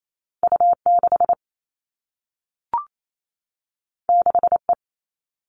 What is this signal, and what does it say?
Morse code, heard as a keyed tone, about 28 wpm, 700 Hz. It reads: U6 6E
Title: All Cut Numbers - 2-Digit 28wpm